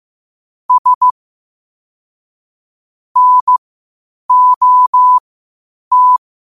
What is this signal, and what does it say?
Morse code, heard as a keyed tone, about 15 wpm, spaced as for 8 wpm, 1000 Hz. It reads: S NOT
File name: qotdfiles / 8wpm